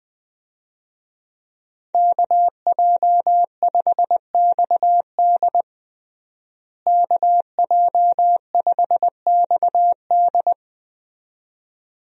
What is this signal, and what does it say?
Morse code, heard as a keyed tone, about 20 wpm, 700 Hz. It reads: KJ5XD KJ5XD
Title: USA Call Signs - 5 Characters - Repeated 2x - 20wpm